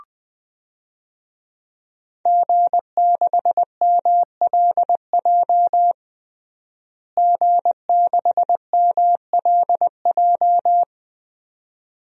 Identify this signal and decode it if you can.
Morse code, heard as a keyed tone, about 21 wpm, 700 Hz. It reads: G6MLJ G6MLJ
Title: DXCC-100 Call Signs - 5 Characters - Repeated 2x - 20wpm